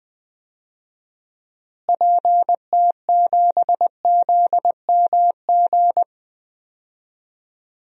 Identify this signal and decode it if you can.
Morse code, heard as a keyed tone, about 20 wpm, 700 Hz. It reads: PT7ZMG
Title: DXCC-100 Call Signs - 6 Characters - Repeated 2x - 20wpm